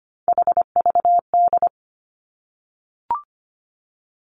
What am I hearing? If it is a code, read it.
Morse code, heard as a keyed tone, about 25 wpm, 700 Hz. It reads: HVD
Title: Three Challenging Letter-Number Combinations 25wpm